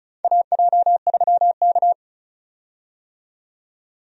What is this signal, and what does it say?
Morse code, heard as a keyed tone, about 36 wpm, 700 Hz. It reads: AJ3K
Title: USA Call Signs - 4 Characters - 35wpm